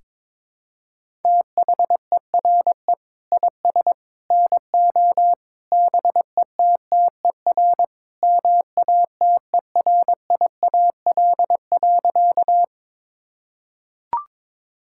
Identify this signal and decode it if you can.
Morse code, heard as a keyed tone, about 22 wpm, 700 Hz. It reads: THERE IS NO BETTER MATERIAL.
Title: Sentences from Top 1000 Words - Review Part 5 - 22wpm